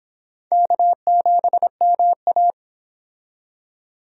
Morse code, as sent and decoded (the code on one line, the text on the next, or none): -.- --... -- .-
K7MA